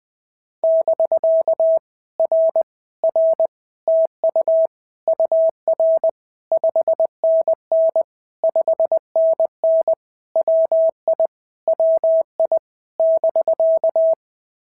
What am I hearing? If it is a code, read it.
Morse code, heard as a keyed tone, about 19 wpm, 650 Hz. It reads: <BK> R R TU UR 5NN 5NN WI WI <BK>
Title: POTA QSO Practice - 20wpm